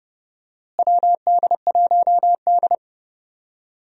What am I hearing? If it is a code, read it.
Morse code, heard as a keyed tone, about 30 wpm, 700 Hz. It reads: WD1D
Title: USA Call Signs - 4 Characters - Repeated 3x - 30wpm